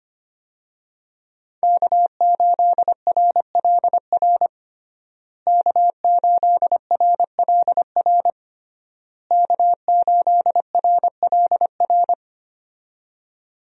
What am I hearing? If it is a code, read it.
Morse code, heard as a keyed tone, about 25 wpm, 700 Hz. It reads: K8RLR K8RLR K8RLR
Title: USA Call Signs - 5 Characters - Repeated 3x - 25wpm